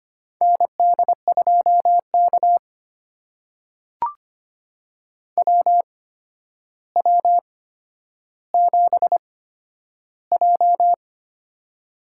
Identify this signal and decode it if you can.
Morse code, heard as a keyed tone, about 25 wpm, spaced as for 11 wpm, 700 Hz. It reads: ND2K WW7J
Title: USA Call Signs - 8x Character Spacing - 25wpm